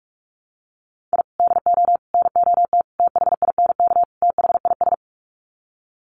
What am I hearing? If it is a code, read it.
Morse code, heard as a keyed tone, about 44 wpm, 700 Hz. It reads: I DO NOT THINK THIS
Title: Sets of 5 Words Encore 45wpm